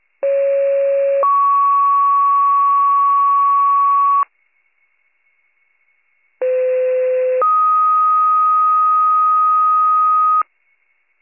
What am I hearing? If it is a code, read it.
Two-tone page, heard as a one-second tone, then a three-second tone, about 550 then 1120 Hz, about 510 then 1250 Hz. It is heard from 0.2 to 10.4 s.